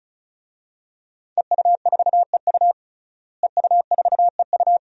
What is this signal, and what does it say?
Morse code, heard as a keyed tone, about 36 wpm, 700 Hz. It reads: EU4EU EU4EU
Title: DXCC-100 Call Signs - 5 Characters - Repeated 2x - 35wpm